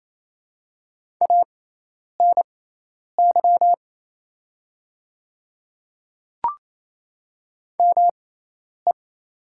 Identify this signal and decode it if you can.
Morse code, heard as a keyed tone, about 28 wpm, spaced as for 10 wpm, 700 Hz. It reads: ANY ME
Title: 6x Character Spacing - Top 300 Words - 28wpm